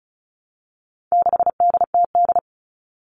Morse code, heard as a keyed tone, about 34 wpm, 700 Hz. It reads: 6DTD
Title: All Cut Numbers - 4-Digit 35wpm